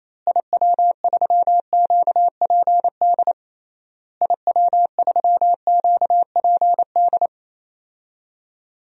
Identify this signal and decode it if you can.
Morse code, heard as a keyed tone, about 28 wpm, 700 Hz. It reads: IW3QPD IW3QPD
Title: DXCC-100 Call Signs - 6 Characters - Repeated 3x - 28wpm